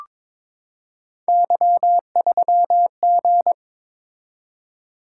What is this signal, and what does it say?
Morse code, heard as a keyed tone, about 22 wpm, 700 Hz. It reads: Y3G
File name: practice / 22wpm / Three Letter-Number Combinations - 22wpm